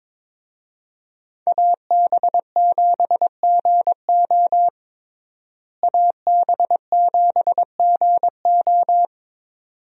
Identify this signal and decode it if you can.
Morse code, heard as a keyed tone, about 22 wpm, 700 Hz. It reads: AB7GO AB7GO
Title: USA Call Signs - 5 Characters - Repeated 2x - 22wpm